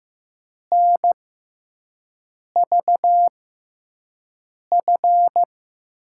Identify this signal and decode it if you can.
Morse code, heard as a keyed tone, about 15 wpm, spaced as for 5 wpm, 700 Hz. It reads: NVF
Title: USA Call Signs - 6 Characters - 6x Character Spacing - 15wpm